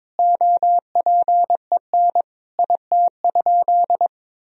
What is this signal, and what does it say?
Morse code, heard as a keyed tone, about 21 wpm, 700 Hz. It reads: OPEN IT?
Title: Sentences from Top 1000 Words - Review Part 4 - 22wpm